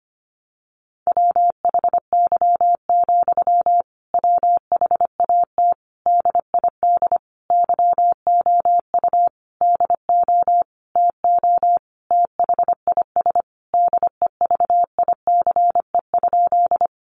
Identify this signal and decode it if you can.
Morse code, heard as a keyed tone, about 25 wpm, 700 Hz. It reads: WHY, WHAT DID YOU DO TO THIS DEVICE?
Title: Sentences from Top 1500 Words 25wpm